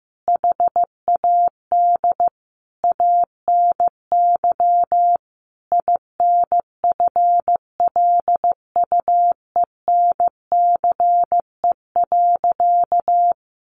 Morse code, heard as a keyed tone, about 15 wpm, 700 Hz. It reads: HAD ANY INFLUENCE.
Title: Sentences from Top 1100 Words 15wpm